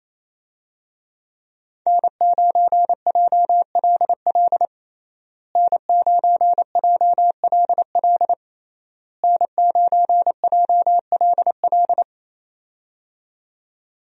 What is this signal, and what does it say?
Morse code, heard as a keyed tone, about 27 wpm, 700 Hz. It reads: N9JLL N9JLL N9JLL
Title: USA Call Signs - 5 Characters - Repeated 3x - 28wpm